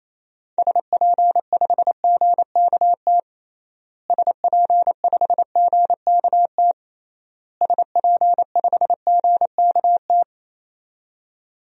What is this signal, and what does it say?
Morse code, heard as a keyed tone, about 28 wpm, 700 Hz. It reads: SP5GKT SP5GKT SP5GKT
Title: DXCC-100 Call Signs - 6 Characters - Repeated 3x - 28wpm